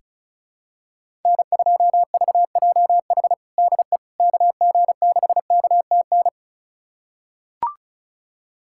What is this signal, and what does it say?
Morse code, heard as a keyed tone, about 36 wpm, 700 Hz. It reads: N2VJH DE KG6KTN
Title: USA Callsign DE Callsign 35wpm